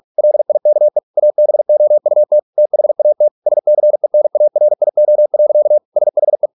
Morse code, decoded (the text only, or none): LIKE ABOUT THAT SCENARIO. SHE